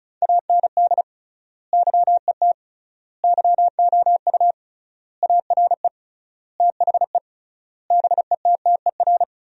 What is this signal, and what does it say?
Morse code, heard as a keyed tone, about 34 wpm, 700 Hz. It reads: AND YET YOU ARE THE BETTER
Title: 3x Word-Spacing - Sentences from Top 500 Words - 35wpm